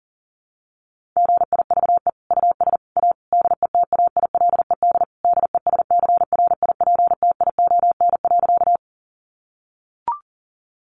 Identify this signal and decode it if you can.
Morse code, heard as a keyed tone, about 40 wpm, 700 Hz. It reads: GIVE US A DETAILED DESCRIPTION.
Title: Sentences from Top 1500 Words 40wpm